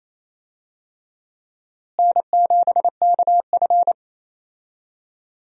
Morse code, heard as a keyed tone, about 28 wpm, 700 Hz. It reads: N7KF
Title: USA Call Signs - 4 Characters - Repeated 3x - 28wpm